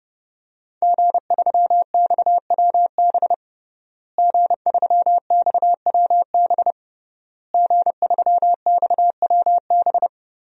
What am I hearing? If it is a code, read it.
Morse code, heard as a keyed tone, about 30 wpm, 700 Hz. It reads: G3XWB G3XWB G3XWB
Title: DXCC-100 Call Signs - 5 Characters - Repeated 3x - 30wpm